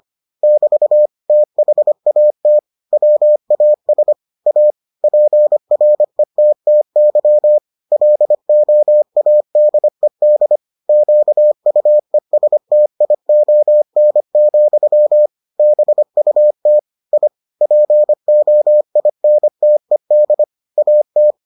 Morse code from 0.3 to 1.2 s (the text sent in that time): <BT>